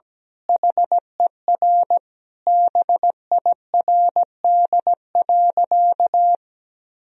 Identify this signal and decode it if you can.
Morse code, heard as a keyed tone, about 17 wpm, 700 Hz. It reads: HER BIRD.
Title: Sentences from Top 1000 Words - Review Part 2 - 17wpm